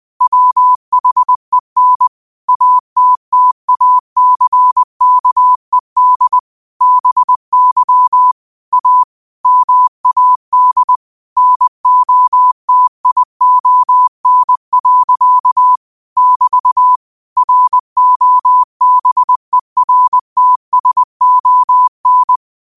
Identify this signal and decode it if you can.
Morse code, heard as a keyed tone, about 20 wpm, 1000 Hz. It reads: WHEN ATTACKED BY A MAD NOTION. <BT> ROBERTSON
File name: qotdfiles / 20wpm